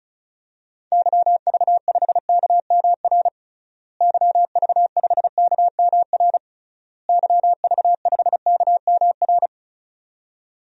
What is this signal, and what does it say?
Morse code, heard as a keyed tone, about 34 wpm, 700 Hz. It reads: YV5KMR YV5KMR YV5KMR
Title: DXCC-100 Call Signs - 6 Characters - Repeated 3x - 35wpm